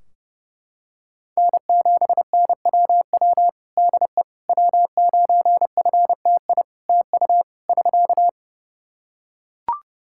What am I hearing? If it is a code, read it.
Morse code, heard as a keyed tone, about 30 wpm, 700 Hz. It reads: N7NWW DE W9FTI TU <SK>
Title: Practice QSO - Exchange 4 - Ending - 30wpm